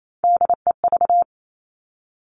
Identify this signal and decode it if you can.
Morse code, heard as a keyed tone, about 28 wpm, 700 Hz. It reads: DEV
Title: Three Challenging Letter-Number Combinations 28wpm